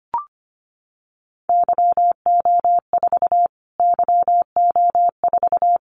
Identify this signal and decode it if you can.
Morse code, heard as a keyed tone, about 25 wpm, 700 Hz. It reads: YO4 YO4
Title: Callsign Prefixes for Top 25 DXCC - Repeated 3x - 25wpm